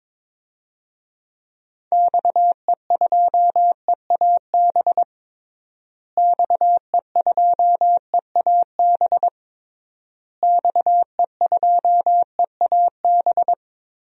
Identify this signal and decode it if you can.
Morse code, heard as a keyed tone, about 22 wpm, 700 Hz. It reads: XE2EAB XE2EAB XE2EAB
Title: DXCC-100 Call Signs - 6 Characters - Repeated 3x - 22wpm